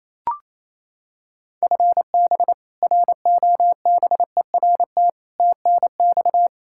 Morse code, heard as a keyed tone, about 28 wpm, 700 Hz. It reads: FB ROBERT TNX
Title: K1USN SST QSO Practice 28wpm